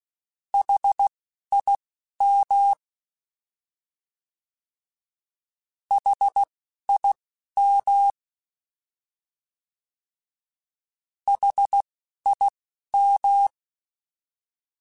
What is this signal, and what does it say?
Morse code, heard as a keyed tone, about 16 wpm, spaced as for 12 wpm, 800 Hz. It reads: HIM HIM HIM